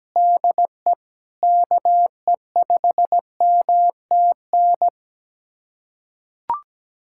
Morse code, heard as a keyed tone, about 17 wpm, 700 Hz. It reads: DE KE5MTN
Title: USA Callsign DE Callsign - 6 Characters - 17wpm